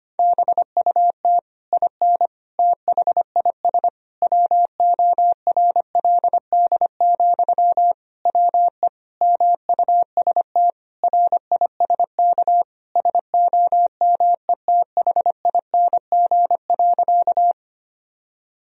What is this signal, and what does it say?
Morse code, heard as a keyed tone, about 25 wpm, 700 Hz. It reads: BUT IN THIS WORLD, WE MUST RISK SOMETHING.
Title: Sentences from Top 1000 Words - Review Part 4 - 25wpm